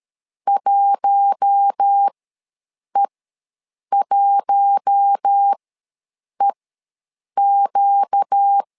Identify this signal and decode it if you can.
Morse code, heard as a keyed tone, about 13 wpm, spaced as for 7 wpm, 800 Hz. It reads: 1E1EQ
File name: LEZIONI / 4 Q1 ws 12wpm 2129